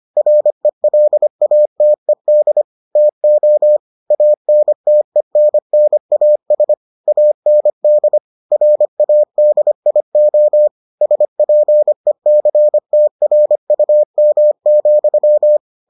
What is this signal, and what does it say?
Morse code, heard as a keyed tone, about 25 wpm, 600 Hz. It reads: RELATED TO ANTENNAS AND RADIO SPECTRUM,